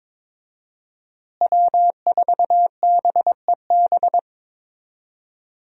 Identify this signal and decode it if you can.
Morse code, heard as a keyed tone, about 22 wpm, 700 Hz. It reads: W4BEB
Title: USA Call Signs - 5 Characters - Repeated 2x - 22wpm